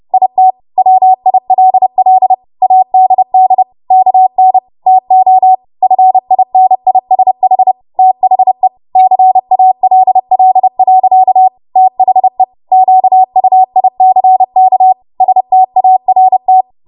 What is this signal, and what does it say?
Morse code, heard as a keyed tone, about 30 wpm, 750 Hz. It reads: IT WILL ADD KN TO FINISH THE CALL. THE QUICK START
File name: Morse / 30WPM